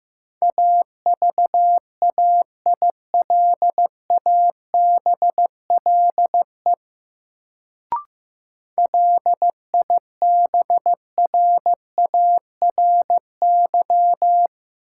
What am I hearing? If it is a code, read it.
Morse code, heard as a keyed tone, about 15 wpm, 700 Hz. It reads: AVAILABLE LIBRARY
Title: Sentences from Top 600 Words - New Word Drill - 15wpm